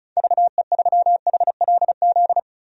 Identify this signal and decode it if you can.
Morse code, heard as a keyed tone, about 34 wpm, 700 Hz. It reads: VE3HLZ
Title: DXCC-100 Call Signs - 6 Characters - 35wpm